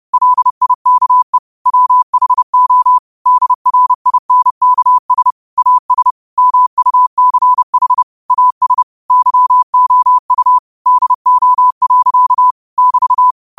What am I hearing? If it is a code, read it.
Morse code, heard as a keyed tone, about 30 wpm, 1000 Hz. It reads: LIKE WHO DRINKS AS MUCH AS YOU DO. <BT>
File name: qotdfiles / 30wpm